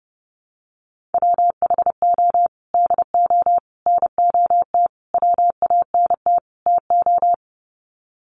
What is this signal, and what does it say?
Morse code, heard as a keyed tone, about 30 wpm, 700 Hz. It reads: WHO DO NOT WANT TO
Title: Sets of 5 Words Encore2 30wpm